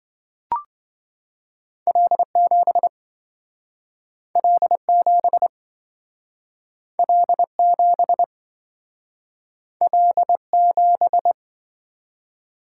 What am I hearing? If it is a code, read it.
Morse code, heard as a keyed tone, about 27 wpm, 700 Hz. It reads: L7 L7 L7 L7